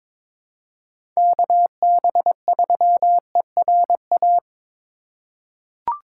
Morse code, heard as a keyed tone, about 22 wpm, 700 Hz. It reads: KB3ERA
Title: USA Call Signs - 6 Characters - 2x Character Spacing - 22wpm